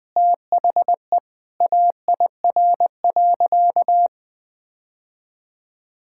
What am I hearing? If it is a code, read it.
Morse code, heard as a keyed tone, about 20 wpm, 700 Hz. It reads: THE AIR.
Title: Sentences with 3-Word Phrases - 20wpm